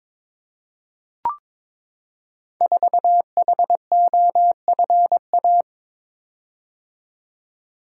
Morse code, heard as a keyed tone, about 22 wpm, 690 Hz. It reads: 4HOFA
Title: Code Groups - 5 Letter:Numbers - 22wpm